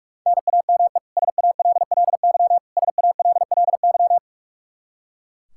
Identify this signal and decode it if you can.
Morse code, heard as a keyed tone, about 43 wpm, 700 Hz. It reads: NAME SALLY SALLY
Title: K1USN SST QSO Practice 45wpm